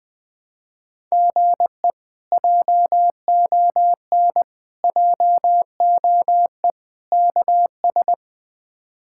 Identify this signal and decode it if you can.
Morse code, heard as a keyed tone, about 20 wpm, 700 Hz. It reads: GE JON JOE KS
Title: K1USN SST QSO Practice - 20wpm